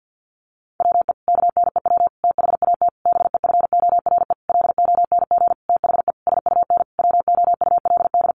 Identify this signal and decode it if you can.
Morse code, heard as a keyed tone, about 50 wpm, 700 Hz. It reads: WE KNEW THAT BEFORE LONG THE SUN WOULD
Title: Sentences from Top 400 Words 50wpm